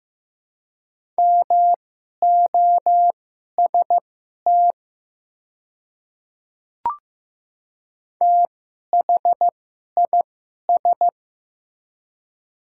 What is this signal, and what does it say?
Morse code, heard as a keyed tone, about 15 wpm, spaced as for 11 wpm, 700 Hz. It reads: MOST THIS
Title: 2x Character Spacing - Top 100 Words - 15wpm